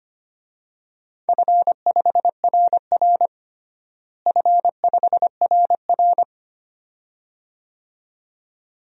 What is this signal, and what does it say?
Morse code, heard as a keyed tone, about 25 wpm, 700 Hz. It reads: F5RR F5RR
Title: DXCC-100 Call Signs - 4 Characters - Repeated 2x - 25wpm